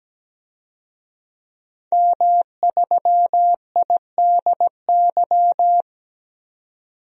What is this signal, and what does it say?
Morse code, heard as a keyed tone, about 17 wpm, 700 Hz. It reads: M3IDY
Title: DXCC-100 Call Signs - 5 Characters - 17wpm